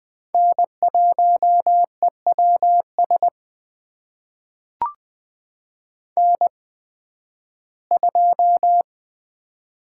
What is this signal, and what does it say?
Morse code, heard as a keyed tone, about 20 wpm, spaced as for 13 wpm, 700 Hz. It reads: N1EWS N2
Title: USA Call Signs - 5 Characters - 8x Character Spacing - 20wpm